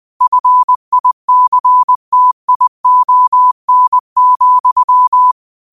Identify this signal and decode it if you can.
Morse code, heard as a keyed tone, about 20 wpm, 1000 Hz. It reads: FICTION,
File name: qotdfiles / 20wpm